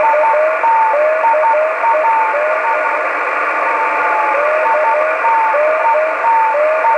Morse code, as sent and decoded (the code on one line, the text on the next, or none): none